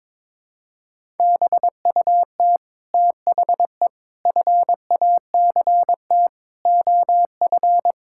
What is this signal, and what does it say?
Morse code, heard as a keyed tone, about 22 wpm, 700 Hz. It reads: BUT THE FACT OF
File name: practice / 22wpm / Sets of 8 Words - 22wpm